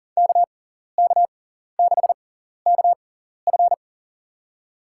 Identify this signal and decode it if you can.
Morse code, heard as a keyed tone, about 39 wpm, spaced as for 14 wpm, 700 Hz. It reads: KK6KF
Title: USA Call Signs - 6x Character Spacing - 40wpm